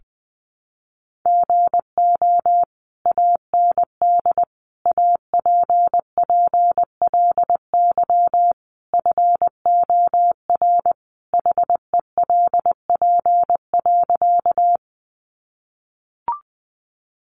Morse code, heard as a keyed tone, about 20 wpm, 700 Hz. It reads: GO AND APPLY FOR HELP.
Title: Sentences from Top 1200 Words 20wpm